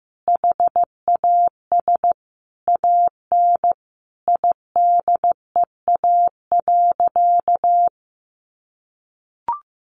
Morse code, heard as a keyed tone, about 15 wpm, 700 Hz. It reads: HAS AN IDEA.
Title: Additional Easy Phrases 15wpm